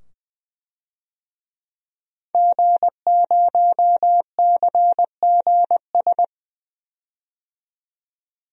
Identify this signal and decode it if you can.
Morse code, heard as a keyed tone, about 20 wpm, 700 Hz. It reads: G0CGS
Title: DXCC-100 Call Signs - 5 Characters - 20wpm